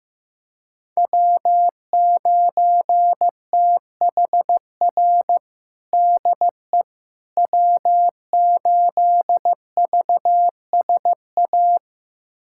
Morse code, heard as a keyed tone, about 15 wpm, 700 Hz. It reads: W9THR DE W8VSA
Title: USA Callsign DE Callsign - 5 Characters - 15wpm